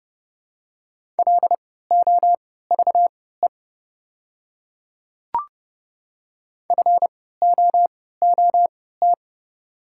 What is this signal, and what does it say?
Morse code, heard as a keyed tone, about 31 wpm, spaced as for 17 wpm, 700 Hz. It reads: LOVE FOOT
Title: 3x Character Spacing - Top 500 Words - 30wpm